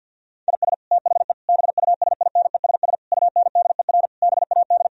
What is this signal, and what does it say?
Morse code, heard as a keyed tone, about 50 wpm, 700 Hz. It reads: IS THE BUSINESS UNDER BAD
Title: Sentences from Top 1000 Words - Review Part 4 - 50wpm